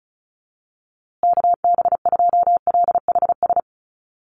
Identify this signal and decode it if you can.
Morse code, heard as a keyed tone, about 34 wpm, 700 Hz. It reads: KB2LHS
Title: USA Call Signs Repeated 3x 35wpm